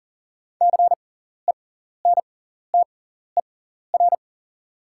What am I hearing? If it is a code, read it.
Morse code, heard as a keyed tone, about 40 wpm, spaced as for 14 wpm, 690 Hz. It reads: CENTER
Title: 6x Character Spacing - Top 500 Words - 40wpm